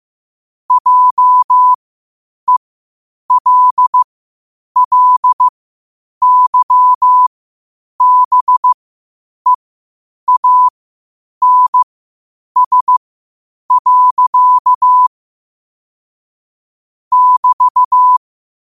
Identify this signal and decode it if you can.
Morse code, heard as a keyed tone, about 15 wpm, spaced as for 8 wpm, 1000 Hz. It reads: JELLYBEANS. <BT>
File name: qotdfiles / 8wpm